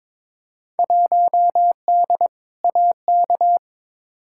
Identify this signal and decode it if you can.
Morse code, heard as a keyed tone, about 22 wpm, 700 Hz. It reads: 1D AK
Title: Field Day Exchanges - 22wpm